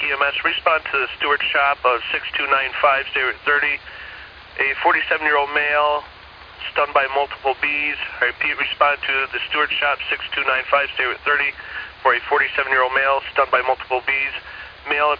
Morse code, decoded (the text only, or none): none